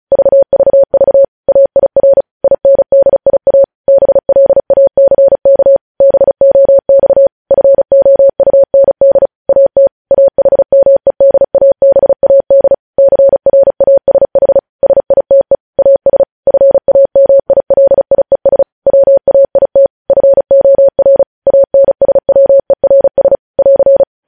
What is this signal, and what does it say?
Morse code, heard as a keyed tone, about 36 wpm, 550 Hz. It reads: VVV AIR INDIA BLACK BOX FOUND AT AHMEDABAD CRASH SITE AS FAMILIES WAIT FOR ANSWERS <AR>
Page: news headlines in morse code at 35 wpm